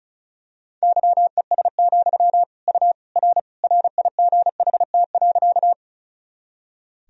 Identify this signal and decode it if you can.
Morse code, heard as a keyed tone, about 36 wpm, 700 Hz. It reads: YES, U R RIGHT.
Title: Sentences from Top 500 Words and Abbreviations 35wpm